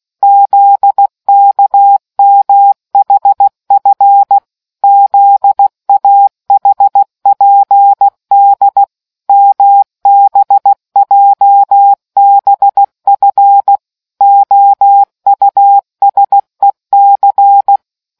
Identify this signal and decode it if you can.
Morse code, heard as a keyed tone, about 16 wpm, 800 Hz. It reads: ZKMHF ZAHPD MBJBF OUSEC